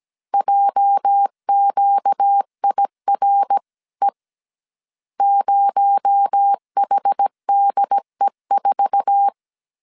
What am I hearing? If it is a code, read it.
Morse code, heard as a keyed tone, about 17 wpm, 800 Hz. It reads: JQIRE 0HDE4
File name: CW / DET / 1 9 1500C S C 16wpm 2909